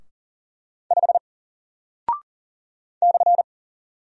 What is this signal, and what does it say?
Morse code, heard as a keyed tone, about 40 wpm, 700 Hz. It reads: 5 /
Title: Single Character (Letter:Number:Punctuation:Prosign) Alt 40wpm